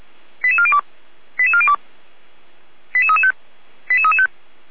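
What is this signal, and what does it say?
Five-tone selective call (ZVEI), heard as quick runs of five tones, tones of about 70 ms. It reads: 88402 88402 88335 88335
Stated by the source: ZVEI five-tone selective call received on SDR, from the Feuerwehr Orlamünde fire brigade server.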